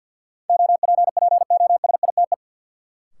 Morse code, heard as a keyed tone, about 53 wpm, 700 Hz. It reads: OPPOSITE
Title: Sentences from Top 1300 Words - New Word Drill - 50wpm